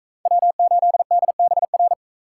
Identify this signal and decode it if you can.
Morse code, heard as a keyed tone, about 42 wpm, 700 Hz. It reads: W8DBR